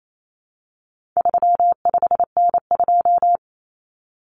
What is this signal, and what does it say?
Morse code, heard as a keyed tone, about 28 wpm, 700 Hz. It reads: 35N2
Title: Common Cut Numbers - 4-Digit 28wpm